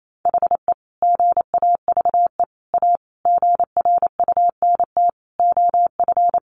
Morse code, heard as a keyed tone, about 28 wpm, 700 Hz. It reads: HE GAVE A GRUNT OF
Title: A Sound Of Thunder 28wpm